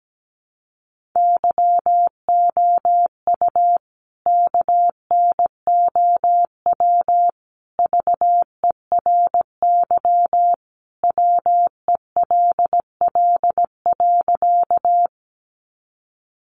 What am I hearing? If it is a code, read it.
Morse code, heard as a keyed tone, about 17 wpm, 700 Hz. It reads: YOU KNOW VERY WELL.
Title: A Sound Of Thunder 17wpm